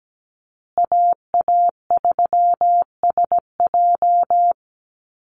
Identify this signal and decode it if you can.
Morse code, heard as a keyed tone, about 17 wpm, 700 Hz. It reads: AA3SJ
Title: USA Call Signs Repeated 3x 17wpm